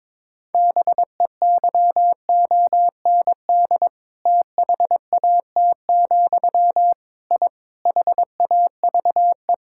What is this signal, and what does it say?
Morse code, heard as a keyed tone, about 22 wpm, 700 Hz. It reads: BEYOND THAT, I HAVE
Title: Sentences from Top 1000 Words - Review Part 6 - 22wpm